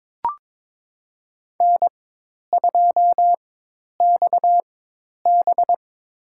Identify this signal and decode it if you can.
Morse code, heard as a keyed tone, about 22 wpm, spaced as for 10 wpm, 700 Hz. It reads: N2XB
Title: USA Call Signs - 4 Characters - 4x Character Spacing - 22wpm